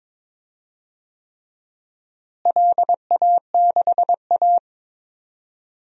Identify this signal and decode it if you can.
Morse code, heard as a keyed tone, about 22 wpm, 700 Hz. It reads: LA6A
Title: DXCC-100 Call Signs - 4 Characters - Repeated 3x - 22wpm